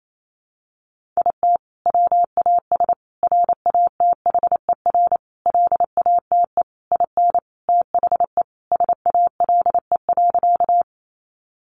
Morse code, heard as a keyed tone, about 28 wpm, 700 Hz. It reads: IT WAS RATHER LATE IN THE SALE.
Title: Sentences from Top 1400 Words 28wpm